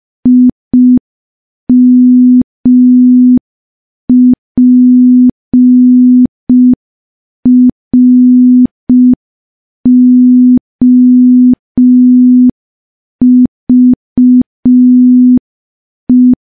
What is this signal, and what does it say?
Morse code, heard as a keyed tone, about 5 wpm, 250 Hz. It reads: IMPROVE
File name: morse / morse-training-05WPM